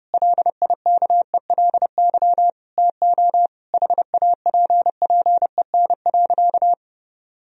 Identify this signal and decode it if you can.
Morse code, heard as a keyed tone, about 32 wpm, 700 Hz. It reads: LIKELY TO HAPPEN.
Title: Sentences from Top 1000 Words - Review Part 4 - 30wpm